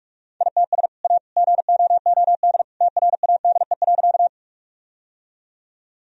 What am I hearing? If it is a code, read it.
Morse code, heard as a keyed tone, about 42 wpm, 700 Hz. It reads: ITS A GOOD TRADE.
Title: Sentences from Top 1000 Words - Review Part 4 - 45wpm